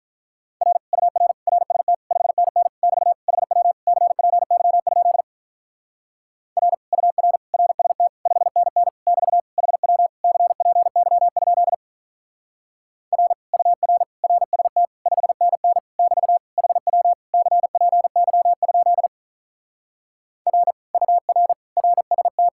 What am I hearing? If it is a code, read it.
Morse code, heard as a keyed tone, about 46 wpm, 700 Hz. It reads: R UR RST 5NN <BT> HW CPY? R UR RST 5NN <BT> HW CPY? R UR RST 5NN <BT> HW CPY? R UR RST